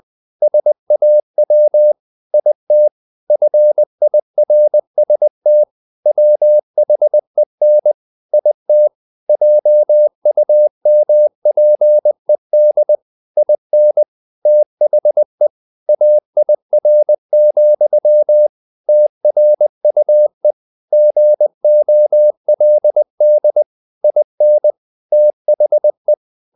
Morse code, heard as a keyed tone, about 20 wpm, 600 Hz. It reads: SAW IT FIRST WHEN IT JUMPED IN THE AIR, TRUE GOLD IN THE
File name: the old man and the sea part 2 20wpm